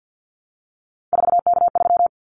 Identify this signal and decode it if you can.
Morse code, heard as a keyed tone, about 50 wpm, 700 Hz. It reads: 4K3
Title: Callsign Prefixes for Top 100 DXCC - 50wpm